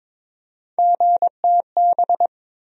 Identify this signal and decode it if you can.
Morse code, heard as a keyed tone, about 21 wpm, 700 Hz. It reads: GTB
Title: USA Call Sign Postfixes - 3 Letters 22wpm